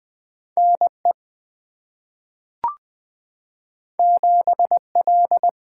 Morse code, heard as a keyed tone, about 20 wpm, 700 Hz. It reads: N E 7 L
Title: Two Letter-Number Combinations - 20wpm